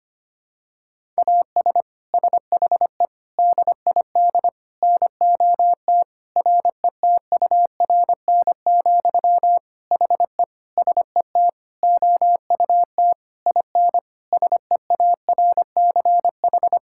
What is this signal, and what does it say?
Morse code, heard as a keyed tone, about 25 wpm, 700 Hz. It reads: AS SHE DID NOT RETURN, HE SET OUT IN SEARCH
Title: Sentences from Top 1000 Words - Review Part 5 - 25wpm